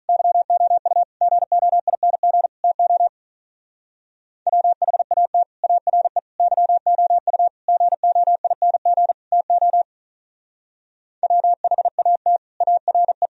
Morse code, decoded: YOU GOING TO WHAT ARE YOU GOING TO WHAT ARE

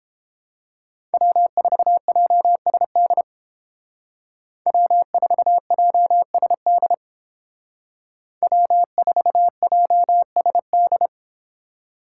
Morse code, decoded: W4JSD W4JSD W4JSD

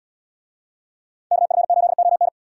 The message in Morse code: KF8ON

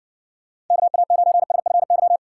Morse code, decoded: KA9SFY

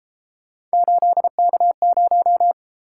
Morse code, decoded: 8K0

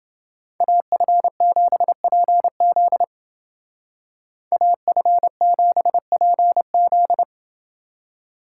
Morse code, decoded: AF7PZ AF7PZ

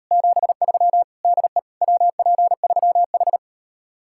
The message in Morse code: 73 DE WP3H

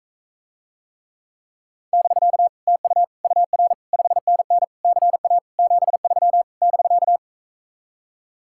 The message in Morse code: <BK> TU UR 5NN CA 73 <BK>

